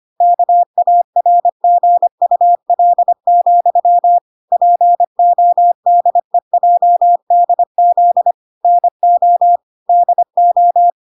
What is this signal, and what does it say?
Morse code, heard as a keyed tone, about 25 wpm, 700 Hz. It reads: KARGUL, PODEJDZ NO DO
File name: 25wpm